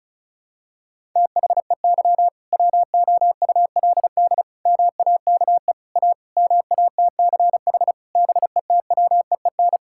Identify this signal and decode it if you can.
Morse code, heard as a keyed tone, about 36 wpm, 700 Hz. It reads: THEY WOULD MAKE A MATCH BETWEEN
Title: Sentences from Top 1600 Words 35wpm